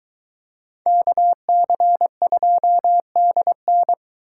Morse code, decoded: KC2DN

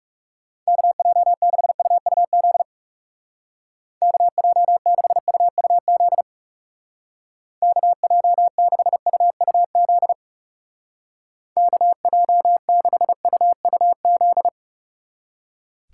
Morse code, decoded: KJ6UUZ KJ6UUZ KJ6UUZ KJ6UUZ

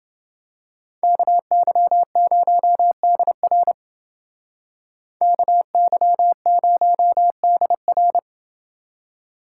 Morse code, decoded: KY0DR KY0DR

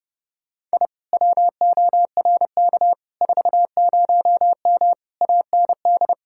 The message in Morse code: I WORK 40M AND

